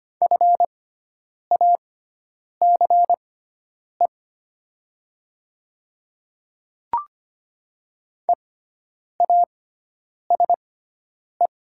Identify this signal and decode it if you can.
Morse code, heard as a keyed tone, about 25 wpm, spaced as for 9 wpm, 690 Hz. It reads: FACE EASE
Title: 6x Character Spacing - Top 500 Words - 25wpm